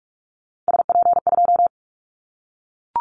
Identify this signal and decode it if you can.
Morse code, heard as a keyed tone, about 44 wpm, 700 Hz. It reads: SP2
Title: Callsign Prefixes for Top 25 DXCC - 45wpm